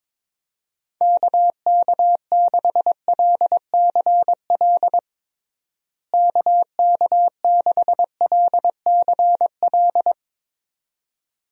Morse code, heard as a keyed tone, about 21 wpm, 700 Hz. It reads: KK6LCL KK6LCL
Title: USA Call Signs - 6 Characters - Repeated 2x - 22wpm